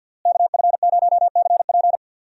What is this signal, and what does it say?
Morse code, heard as a keyed tone, about 47 wpm, 700 Hz. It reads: KF0CP
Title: USA Call Signs - 5 Characters - Repeated 3x - 50wpm